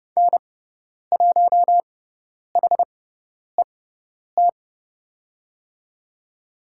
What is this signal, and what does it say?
Morse code, heard as a keyed tone, about 30 wpm, spaced as for 10 wpm, 700 Hz. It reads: N1HET
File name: Lesson 18g - Letter m - Call Signs 30wpm